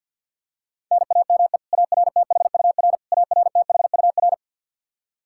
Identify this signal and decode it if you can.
Morse code, heard as a keyed tone, about 49 wpm, 700 Hz. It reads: NAME ARTHUR ARTHUR
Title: Practice QSO - Exchange 1 - Beginning - 50wpm